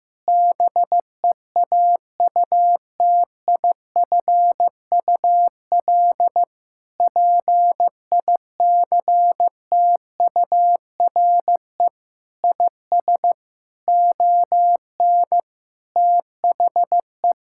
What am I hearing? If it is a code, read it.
Morse code, heard as a keyed tone, about 15 wpm, 700 Hz. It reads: BEAUTIFUL PICTURE IS ON THE
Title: Sentences from Top 1000 Words - Review Part 6 - 15wpm